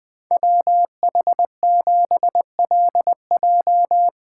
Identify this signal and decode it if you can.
Morse code, heard as a keyed tone, about 20 wpm, 700 Hz. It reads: WH7LJ
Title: Speed Racing - USA 2x2 Call Signs 17wpm